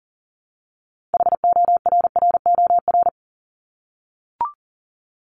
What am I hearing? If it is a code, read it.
Morse code, heard as a keyed tone, about 39 wpm, 700 Hz. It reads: HORROR
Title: Top 1501-1750 Words 40wpm